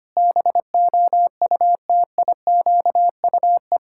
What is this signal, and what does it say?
Morse code, heard as a keyed tone, about 25 wpm, 700 Hz. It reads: BOUTIQUE
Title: Pangram Challenge v2 - 25wpm